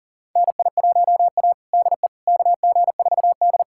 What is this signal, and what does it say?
Morse code, heard as a keyed tone, about 40 wpm, 700 Hz. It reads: NI1A DE KG4D
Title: USA Callsign DE Callsign - 4 Characters - 40wpm